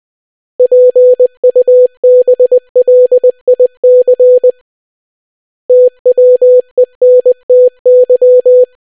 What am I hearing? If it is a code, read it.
Morse code, heard as a keyed tone, about 20 wpm, 500 Hz. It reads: PUBLIC TWENTY